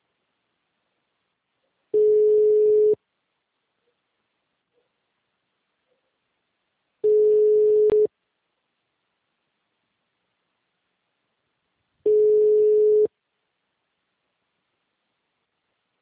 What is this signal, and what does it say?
Ringback tone (Europe, 425 Hz).